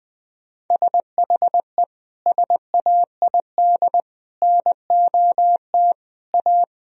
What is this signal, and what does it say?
Morse code, heard as a keyed tone, about 20 wpm, 700 Hz. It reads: SHE SAID NOT A
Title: Sentences from Top 1000 Words - Review Part 4 - 20wpm